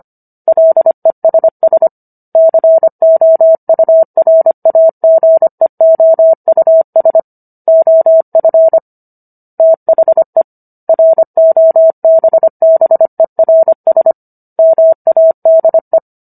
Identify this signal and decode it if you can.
Morse code, heard as a keyed tone, about 25 wpm, 650 Hz. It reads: LESS COURAGEOUS OF THE ROBBERS MADE